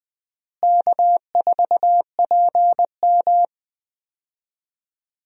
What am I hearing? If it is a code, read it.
Morse code, heard as a keyed tone, about 20 wpm, 700 Hz. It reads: K4PM
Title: USA Call Signs - 4 Characters - Repeated 3x - 20wpm